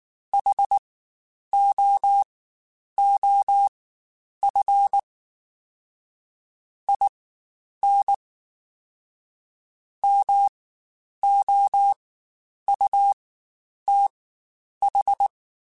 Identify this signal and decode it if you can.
Morse code, heard as a keyed tone, about 19 wpm, spaced as for 9 wpm, 800 Hz. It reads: HOOF IN MOUTH
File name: Quotes Spaced / Quotes Spaced-mp3s / HoofInMouth sp @18